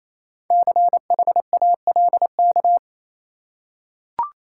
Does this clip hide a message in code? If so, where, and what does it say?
Yes: Morse code from 0.5 to 2.8 s, heard as a keyed tone, about 28 wpm, 700 Hz. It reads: CHALK